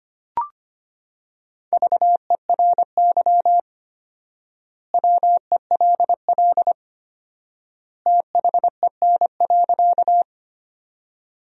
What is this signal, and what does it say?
Morse code, heard as a keyed tone, about 25 wpm, 700 Hz. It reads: VERY WELL THEN.
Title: 4x Word-Spacing - Sentences from Top 200 Words - 25wpm